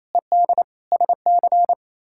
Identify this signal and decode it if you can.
Morse code, heard as a keyed tone, about 28 wpm, 700 Hz. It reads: ED SC
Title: K1USN SST QSO Practice - 28wpm